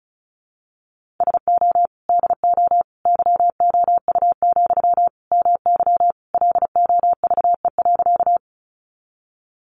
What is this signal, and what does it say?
Morse code, heard as a keyed tone, about 34 wpm, 700 Hz. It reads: SO DO YOU, MY LOVE.
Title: Sentences from Top 400 Words 35wpm